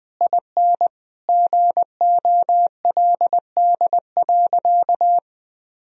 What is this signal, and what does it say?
Morse code, heard as a keyed tone, about 19 wpm, 700 Hz. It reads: IN GOLD.
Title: Sentences from Top 1000 Words - Review Part 3 - 20wpm